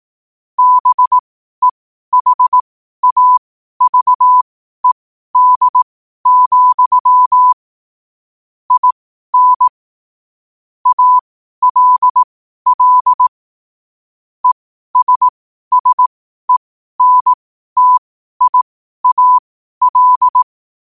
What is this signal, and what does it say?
Morse code, heard as a keyed tone, about 18 wpm, spaced as for 12 wpm, 1000 Hz. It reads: BEHAVED, IN ALL ESSENTIAL
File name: qotdfiles / 12wpm